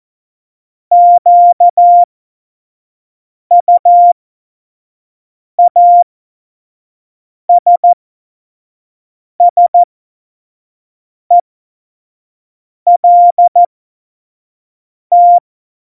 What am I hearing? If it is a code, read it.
Morse code, heard as a keyed tone, about 14 wpm, spaced as for 5 wpm, 700 Hz. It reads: QUASSELT
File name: Morsekurs / Witze / Witz 19 Mai 2020, 25 F70